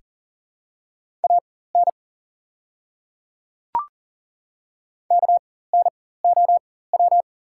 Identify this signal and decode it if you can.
Morse code, heard as a keyed tone, about 40 wpm, spaced as for 19 wpm, 690 Hz. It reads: AN KNOW
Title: 4x Character Spacing - Top 100 Words - 40wpm